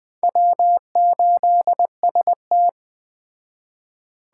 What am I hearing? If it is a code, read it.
Morse code, heard as a keyed tone, about 21 wpm, 700 Hz. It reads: W8ST